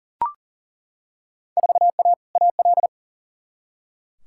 Morse code, heard as a keyed tone, about 37 wpm, 700 Hz. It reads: 4A AL
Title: Field Day Exchanges - 40wpm